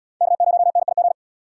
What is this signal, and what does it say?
Morse code, heard as a keyed tone, about 77 wpm, 700 Hz. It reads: K0NIG